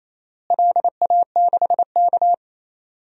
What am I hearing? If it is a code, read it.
Morse code, heard as a keyed tone, about 28 wpm, 700 Hz. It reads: LA6K
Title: DXCC-100 Call Signs - 4 Characters - Repeated 2x - 28wpm